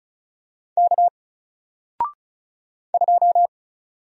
Morse code, heard as a keyed tone, about 36 wpm, 700 Hz. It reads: K 2
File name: Single-Letter-Number-35wpm